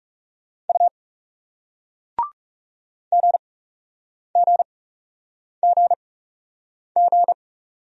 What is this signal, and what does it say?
Morse code, heard as a keyed tone, about 39 wpm, spaced as for 8 wpm, 700 Hz. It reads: U GGGG